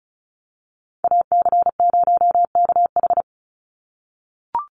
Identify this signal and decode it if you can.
Morse code, heard as a keyed tone, about 34 wpm, 700 Hz. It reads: AC0KH
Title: USA Call Signs 35wpm